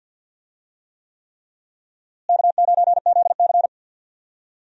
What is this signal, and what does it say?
Morse code, heard as a keyed tone, about 49 wpm, 700 Hz. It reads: K9ZC
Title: USA Call Signs - 4 Characters - Repeated 3x - 50wpm